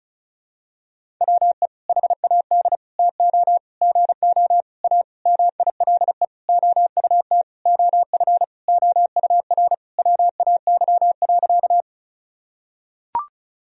Morse code, heard as a keyed tone, about 36 wpm, 700 Hz. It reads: WE HAD TO GO A MILE OUT OF OUR WAY.
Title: Sentences from Top 1000 Words - Review Part 1 - 35wpm